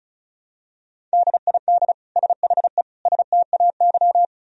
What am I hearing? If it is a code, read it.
Morse code, heard as a keyed tone, about 36 wpm, 700 Hz. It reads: DID SHE STAY
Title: Sentences from Top 500 Words and Abbreviations 35wpm